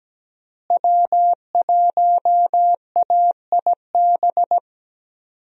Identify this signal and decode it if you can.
Morse code, heard as a keyed tone, about 17 wpm, 700 Hz. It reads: W1AIB
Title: USA Call Signs - 5 Characters - Repeated 2x - 17wpm